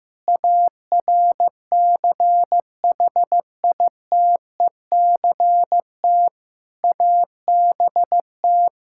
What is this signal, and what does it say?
Morse code, heard as a keyed tone, about 15 wpm, 700 Hz. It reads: ARCHITECT ABT
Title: Practice QSO - Exchange 3 - More Info - 15wpm